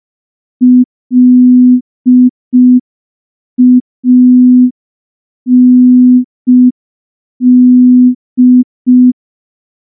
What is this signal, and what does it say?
Morse code, heard as a keyed tone, about 5 wpm, 250 Hz. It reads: LAND
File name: morse / morse-training-05WPM